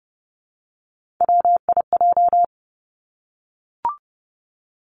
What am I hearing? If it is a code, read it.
Morse code, heard as a keyed tone, about 31 wpm, 700 Hz. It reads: WIJ
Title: Three Challenging Letter-Number Combinations 30wpm